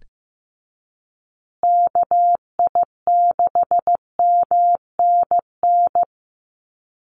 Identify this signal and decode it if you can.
Morse code, heard as a keyed tone, about 15 wpm, 700 Hz. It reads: KI6MNN
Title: USA Call Signs Repeated 2x 15wpm